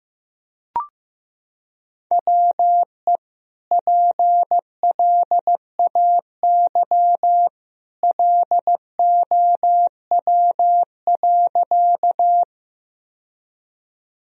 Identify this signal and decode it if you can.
Morse code, heard as a keyed tone, about 16 wpm, 700 Hz. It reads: WE PLAY LOW.
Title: Sentences from Top 1000 Words - Review Part 1 - 15wpm